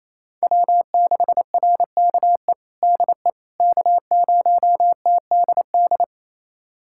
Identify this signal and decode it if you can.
Morse code, heard as a keyed tone, about 27 wpm, 700 Hz. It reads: W6RKE DE K0TDD
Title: USA Callsign DE Callsign - 5 Characters - 28wpm